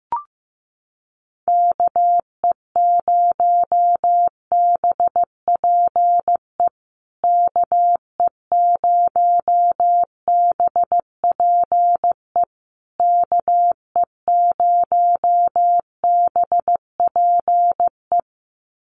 Morse code, heard as a keyed tone, about 15 wpm, 700 Hz. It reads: KE0BPE KE0BPE KE0BPE
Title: USA Call Signs Repeated 3x 15wpm